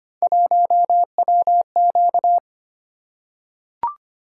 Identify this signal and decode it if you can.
Morse code, heard as a keyed tone, about 25 wpm, 700 Hz. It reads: 1WQ
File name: Three Letter-Number Combinations - 25wpm